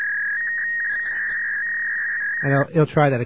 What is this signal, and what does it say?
Fax machine handshake, with speech.